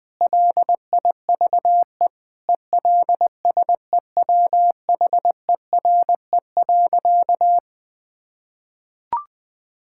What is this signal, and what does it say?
Morse code, heard as a keyed tone, about 20 wpm, 700 Hz. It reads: LIVE ELSEWHERE.
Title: Sentences from Top 1600 Words 20wpm